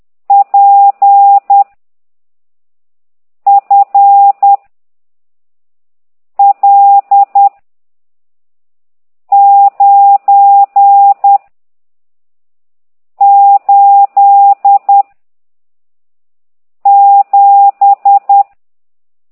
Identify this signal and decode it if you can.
Morse code, heard as a keyed tone, about 10 wpm, 800 Hz. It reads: P F L 9 8 7
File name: ct2idw / exerc15